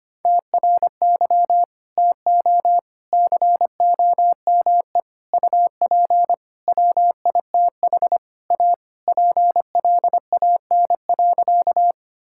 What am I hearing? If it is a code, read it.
Morse code, heard as a keyed tone, about 25 wpm, 700 Hz. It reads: TRY TO COME UP WITH A PLAN.
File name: practice / 25wpm / Sentences with 4-Word Phrases - 25wpm